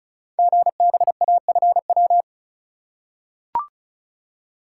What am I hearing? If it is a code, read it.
Morse code, heard as a keyed tone, about 36 wpm, 700 Hz. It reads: GBAFW
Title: Code Groups - 5 Letters - 35wpm